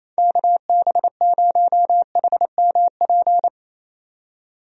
Morse code, heard as a keyed tone, about 27 wpm, 700 Hz. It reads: KB0HMP
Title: USA Call Signs - 6 Characters - 28wpm